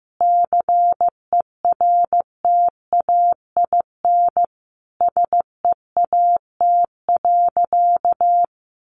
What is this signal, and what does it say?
Morse code, heard as a keyed tone, about 15 wpm, 700 Hz. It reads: CERTAIN SEAT.
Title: Sentences from Top 1500 Words 15wpm